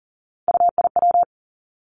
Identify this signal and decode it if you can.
Morse code, heard as a keyed tone, about 41 wpm, 700 Hz. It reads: UIW
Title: Three Challenging Letter-Number Combinations 40wpm